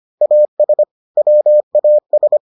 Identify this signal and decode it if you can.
Morse code, heard as a keyed tone, about 25 wpm, 600 Hz. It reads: AS WAS